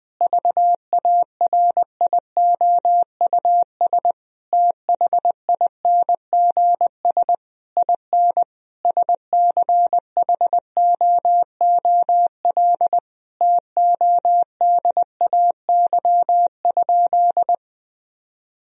Morse code, heard as a keyed tone, about 20 wpm, 700 Hz. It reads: VARIOUS THINGS IN SCHOOL TODAY?
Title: Sentences from Top 1000 Words - Review Part 4 - 20wpm